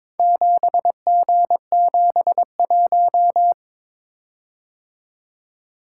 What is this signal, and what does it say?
Morse code, heard as a keyed tone, about 23 wpm, 700 Hz. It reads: 7G71
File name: practice / 22wpm / Four Letter-Number Combinations - 22wpm